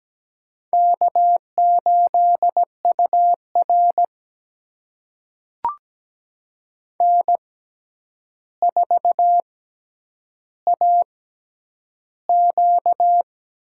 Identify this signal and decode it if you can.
Morse code, heard as a keyed tone, about 17 wpm, spaced as for 9 wpm, 700 Hz. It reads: K8UR N4AQ
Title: USA Call Signs - 4 Characters - 6x Character Spacing - 17wpm